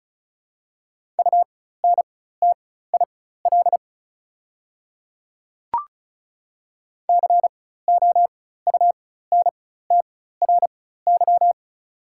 Morse code, heard as a keyed tone, about 36 wpm, spaced as for 16 wpm, 690 Hz. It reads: UNTIL COUNTRY
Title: 4x Character Spacing - Top 300 Words - 35wpm